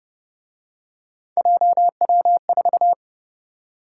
Morse code, heard as a keyed tone, about 29 wpm, 700 Hz. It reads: JW4